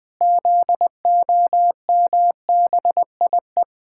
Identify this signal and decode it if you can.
Morse code, heard as a keyed tone, about 20 wpm, 700 Hz. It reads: ZOMBIE
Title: Pangram Challenge v2 - 20wpm